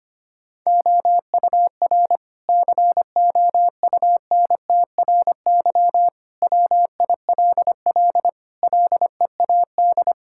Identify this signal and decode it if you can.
Morse code, heard as a keyed tone, about 25 wpm, 700 Hz. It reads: OUR COUNTRY WILL LEAD
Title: Sentences from Top 1000 Words - Review Part 5 - 25wpm